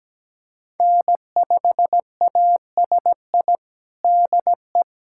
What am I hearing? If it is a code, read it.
Morse code, heard as a keyed tone, about 17 wpm, 700 Hz. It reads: N5ASI DE
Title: USA Callsign DE Callsign 17wpm